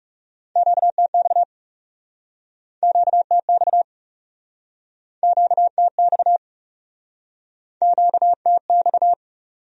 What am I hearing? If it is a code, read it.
Morse code, heard as a keyed tone, about 40 wpm, 700 Hz. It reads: QTX QTX QTX QTX